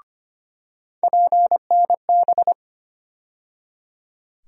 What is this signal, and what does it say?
Morse code, heard as a keyed tone, about 25 wpm, 700 Hz. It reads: PNB
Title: USA Call Sign Postfixes - 3 Letters 25wpm